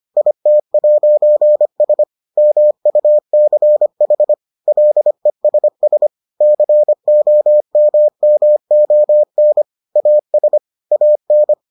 Morse code, heard as a keyed tone, about 25 wpm, 600 Hz. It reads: IT'S MUCH LESS COMMON AS AN